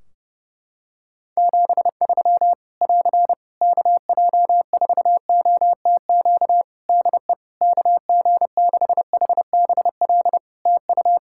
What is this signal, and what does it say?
Morse code, heard as a keyed tone, about 30 wpm, 700 Hz. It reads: 73 <AR> KJ4OTQ DE KG6HBL TU
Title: Practice QSO - Exchange 4 - Ending - 30wpm